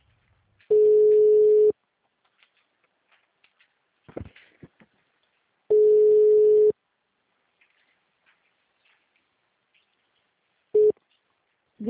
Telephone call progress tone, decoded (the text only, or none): ringback tone